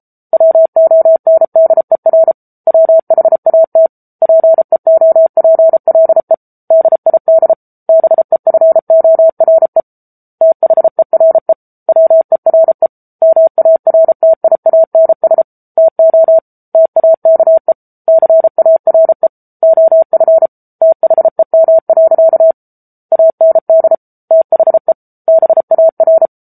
Morse code, heard as a keyed tone, about 34 wpm, 650 Hz. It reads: WONDER WHAT PEOPLE DID BEFORE THERE WERE MARTIANS TO TAKE CARE OF THEM. AND THE BAR
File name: Book Two / 07-Chapter 7 - 33 WPM